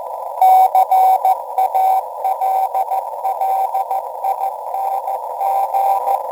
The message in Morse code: CALLING